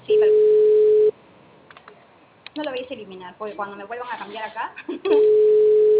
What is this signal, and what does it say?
Ringback tone (Europe, 425 Hz), with speech.